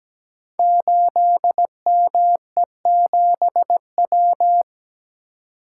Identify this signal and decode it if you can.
Morse code, heard as a keyed tone, about 17 wpm, 700 Hz. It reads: 8ME7W
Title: Code Groups - 5 Letter:Numbers - 17wpm